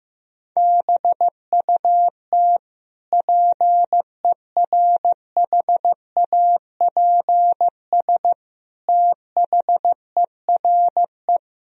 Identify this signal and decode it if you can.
Morse code, heard as a keyed tone, about 15 wpm, 700 Hz. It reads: BUT PERHAPS THERE
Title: Sentences from Top 1000 Words - Review Part 3 - 15wpm